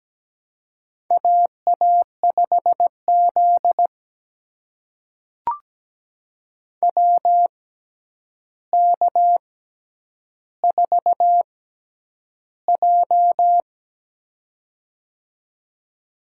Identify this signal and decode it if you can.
Morse code, heard as a keyed tone, about 17 wpm, spaced as for 9 wpm, 700 Hz. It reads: AA5Z WK4J
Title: USA Call Signs - 4 Characters - 6x Character Spacing - 17wpm